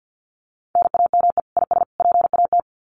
Morse code, heard as a keyed tone, about 50 wpm, 700 Hz. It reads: NAME IS PAT
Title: Common QSO Phrases 50wpm